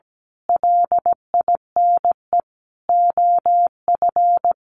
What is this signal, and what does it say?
Morse code, heard as a keyed tone, about 17 wpm, 700 Hz. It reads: LINE OF
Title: A Sound Of Thunder 17wpm